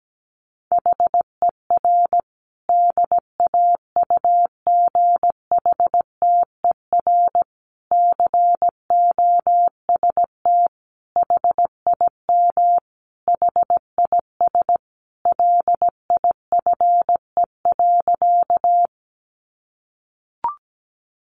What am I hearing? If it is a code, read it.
Morse code, heard as a keyed tone, about 17 wpm, 700 Hz. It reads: HER DAUGHTER COST HIM HIS LIFE.
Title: Sentences from Top 1500 Words 17wpm